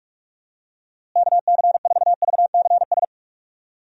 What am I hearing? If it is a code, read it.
Morse code, heard as a keyed tone, about 46 wpm, 700 Hz. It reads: KC4VCS
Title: USA Call Signs - 6 Characters - Repeated 2x - 45wpm